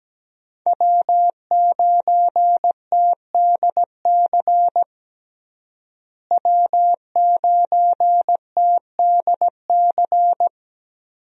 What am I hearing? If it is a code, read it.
Morse code, heard as a keyed tone, about 17 wpm, 700 Hz. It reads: W9TDC W9TDC
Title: USA Call Signs - 5 Characters - Repeated 2x - 17wpm